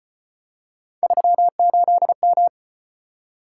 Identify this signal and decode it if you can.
Morse code, heard as a keyed tone, about 35 wpm, 700 Hz. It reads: 38M